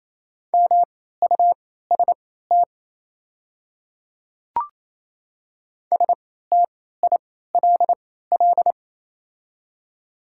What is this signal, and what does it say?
Morse code, heard as a keyed tone, about 28 wpm, spaced as for 16 wpm, 700 Hz. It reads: MUST STILL
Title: 3x Character Spacing - Top 300 Words - 28wpm